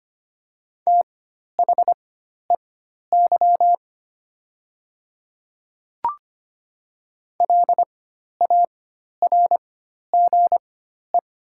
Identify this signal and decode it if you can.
Morse code, heard as a keyed tone, about 25 wpm, spaced as for 12 wpm, 700 Hz. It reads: THEY LARGE
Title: 4x Character Spacing - Top 200 Words - 25wpm